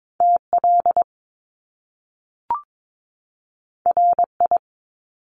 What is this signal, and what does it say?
Morse code, heard as a keyed tone, about 22 wpm, 700 Hz. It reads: TL RI
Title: Top Two Letter Combinations in Words 22wpm